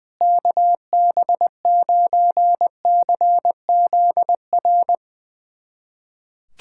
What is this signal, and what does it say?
Morse code, heard as a keyed tone, about 20 wpm, 700 Hz. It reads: KB9CZR
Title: USA Callsign DE Callsign - 6 Characters - 20wpm